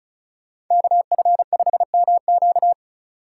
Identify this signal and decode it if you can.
Morse code, heard as a keyed tone, about 36 wpm, 700 Hz. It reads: KF5MQ
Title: Speed Racing - USA 2x2 Call Signs 35wpm